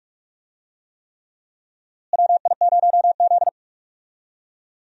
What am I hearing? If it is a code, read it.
Morse code, heard as a keyed tone, about 46 wpm, 700 Hz. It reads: WI0Z
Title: USA Call Signs - 4 Characters - Repeated 3x - 45wpm